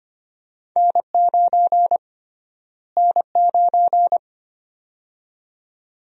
Morse code, heard as a keyed tone, about 25 wpm, 700 Hz. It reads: N9 N9
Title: USA Call Sign Prefixes - Repeated 2x - 25wpm